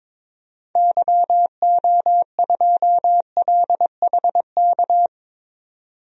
Morse code, heard as a keyed tone, about 22 wpm, 700 Hz. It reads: YO2LHK
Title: DXCC-100 Call Signs - 6 Characters - 22wpm